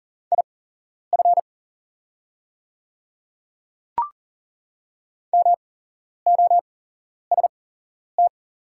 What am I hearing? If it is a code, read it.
Morse code, heard as a keyed tone, about 40 wpm, spaced as for 11 wpm, 700 Hz. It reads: IF MOST
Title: 8x Character Spacing - Top 100 Words - 40wpm